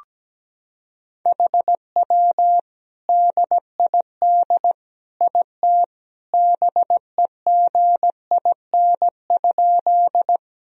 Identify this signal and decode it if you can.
Morse code, heard as a keyed tone, about 17 wpm, 700 Hz. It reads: HW DID IT BEGIN?
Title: Sentences from Top 300 Words and Abbreviations 17wpm